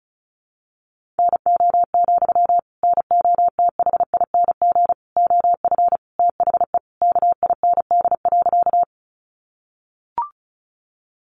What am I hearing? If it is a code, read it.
Morse code, heard as a keyed tone, about 34 wpm, 700 Hz. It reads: NO, NOTHING OF THE KIND.
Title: Sentences from Top 500 Words 35wpm